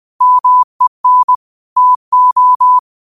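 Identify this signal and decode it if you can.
Morse code, heard as a keyed tone, about 20 wpm, 1000 Hz. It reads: MEN TO
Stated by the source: Morse quote practice at 20 wpm.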